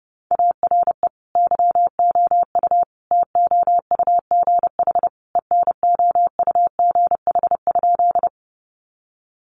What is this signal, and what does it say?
Morse code, heard as a keyed tone, about 30 wpm, 700 Hz. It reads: ARE YOU TOUGH ENOUGH?
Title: Sentences from Top 1200 Words 30wpm